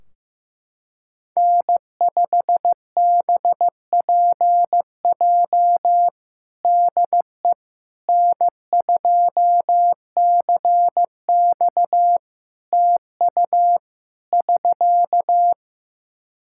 Morse code, heard as a keyed tone, about 15 wpm, 700 Hz. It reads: N5BPJ DE N2CX TU <SK>
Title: K1USN SST QSO Practice 15wpm